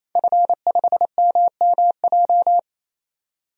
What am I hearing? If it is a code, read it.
Morse code, heard as a keyed tone, about 28 wpm, 700 Hz. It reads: F5MMJ
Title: DXCC-100 Call Signs - 5 Characters - 28wpm